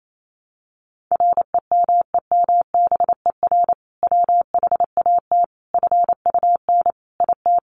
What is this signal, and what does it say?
Morse code, heard as a keyed tone, about 28 wpm, 700 Hz. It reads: REMEMBER WHAT FUN IT
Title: Sentences from Top 700 Words 28wpm